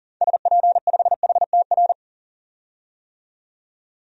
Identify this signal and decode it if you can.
Morse code, heard as a keyed tone, about 40 wpm, 700 Hz. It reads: SP5HTR
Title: DXCC-100 Call Signs - 6 Characters - 40wpm